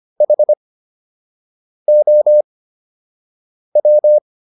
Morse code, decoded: HOW

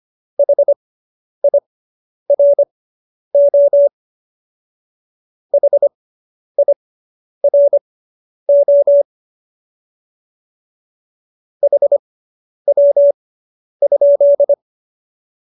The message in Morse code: HIRO HIRO HW?

